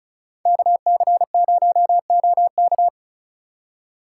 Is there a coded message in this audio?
Yes: Morse code from 0.5 to 2.9 s, heard as a keyed tone, about 46 wpm, 700 Hz. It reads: KC0OK